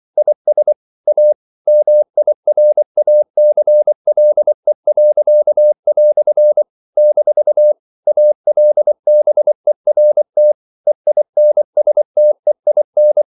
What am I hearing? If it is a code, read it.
Morse code, heard as a keyed tone, about 24 wpm, 600 Hz. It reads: IS A MIRACLE." - ALBERT EINSTEIN
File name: CW / Practice / EnglishQuotations0424